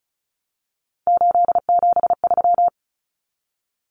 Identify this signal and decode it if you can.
Morse code, heard as a keyed tone, about 34 wpm, 700 Hz. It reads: 873
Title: Numbers - 3-Digit 35wpm